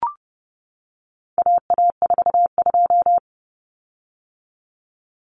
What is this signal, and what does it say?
Morse code, heard as a keyed tone, about 30 wpm, 700 Hz. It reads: AA42
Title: Common Cut Numbers - 4-Digit 30wpm